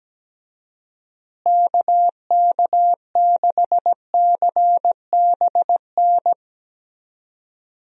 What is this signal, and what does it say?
Morse code, heard as a keyed tone, about 17 wpm, 700 Hz. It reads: KK6CBN
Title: USA Call Signs - 6 Characters - Repeated 2x - 17wpm